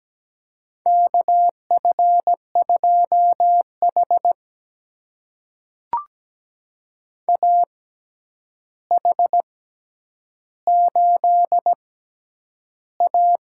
Morse code, heard as a keyed tone, about 17 wpm, spaced as for 9 wpm, 700 Hz. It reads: KF2H AH8A
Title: USA Call Signs - 4 Characters - 6x Character Spacing - 17wpm